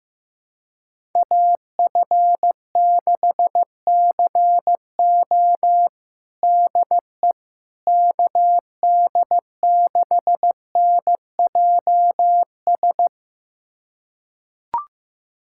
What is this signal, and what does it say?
Morse code, heard as a keyed tone, about 15 wpm, 700 Hz. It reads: AF6CO DE KD6NJS
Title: USA Callsign DE Callsign 15wpm